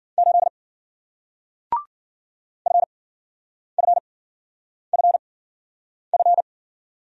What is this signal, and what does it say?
Morse code, heard as a keyed tone, about 53 wpm, spaced as for 9 wpm, 700 Hz. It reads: 8 FFFF